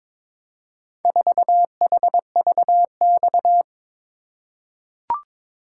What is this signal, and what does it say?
Morse code, heard as a keyed tone, about 22 wpm, 700 Hz. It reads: 4HVX
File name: practice / 22wpm / Four Letter-Number Combinations - 22wpm